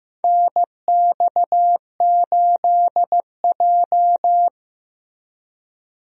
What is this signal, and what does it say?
Morse code, heard as a keyed tone, about 15 wpm, 700 Hz. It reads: NX8J
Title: USA Call Signs - 4 Characters - Repeated 3x - 15wpm